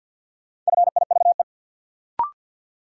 Morse code, heard as a keyed tone, about 53 wpm, 700 Hz. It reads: FIVE
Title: Sentences from Top 400 Words - New Word Drill - 50wpm